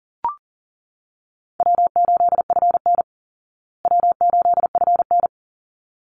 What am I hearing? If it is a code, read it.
Morse code, heard as a keyed tone, about 40 wpm, 700 Hz. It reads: W8FN W8FN
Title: Easy Call Signs Repeated 2x - 40wpm